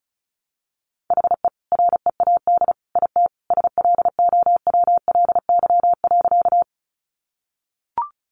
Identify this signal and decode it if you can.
Morse code, heard as a keyed tone, about 34 wpm, 700 Hz. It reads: HE READ IT SLOWLY.
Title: Sentences from Top 1100 Words 35wpm